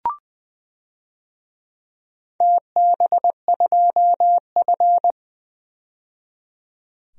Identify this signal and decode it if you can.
Morse code, heard as a keyed tone, about 20 wpm, 700 Hz. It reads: TB2F
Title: DXCC-100 Call Signs - 4 Characters - 20wpm